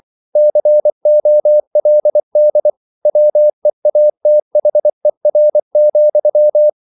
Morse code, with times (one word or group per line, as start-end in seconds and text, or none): 0.4-2.7 COLD
3.1-6.7 WEATHER,